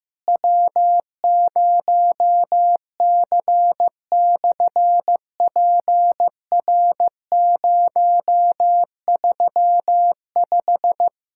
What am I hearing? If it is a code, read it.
Morse code, heard as a keyed tone, about 15 wpm, 700 Hz. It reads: W0C/PR035
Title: SOTA QSO Practice 15wpm